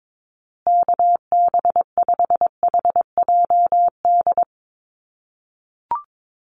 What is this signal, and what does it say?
Morse code, heard as a keyed tone, about 22 wpm, 700 Hz. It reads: KB5HJD
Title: USA Call Signs Repeated 2x 22wpm